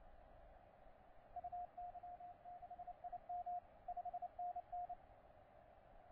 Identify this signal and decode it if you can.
Morse code, heard as a keyed tone, about 28 wpm, 710 Hz. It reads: UY6IM 5NN
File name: C82DX 12CW